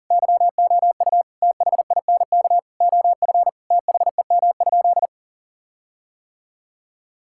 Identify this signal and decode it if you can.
Morse code, heard as a keyed tone, about 37 wpm, 700 Hz. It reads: YOU THINK OF THEM?
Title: Sentences from Top 1600 Words 40wpm